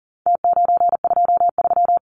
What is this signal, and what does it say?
Morse code, heard as a keyed tone, about 40 wpm, 700 Hz. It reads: T923
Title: Numbers - Contest Style - 4-Digit 40wpm